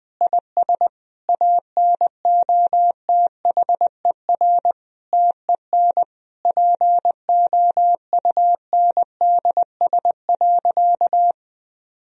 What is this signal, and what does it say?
Morse code, heard as a keyed tone, about 20 wpm, 700 Hz. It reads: IS ANOTHER TEN POUNDS.
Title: Sentences from Top 1000 Words - Review Part 4 - 20wpm